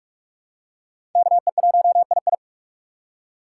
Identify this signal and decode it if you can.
Morse code, heard as a keyed tone, about 44 wpm, 700 Hz. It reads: KE1II
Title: USA Call Signs - 6x Character Spacing - 45wpm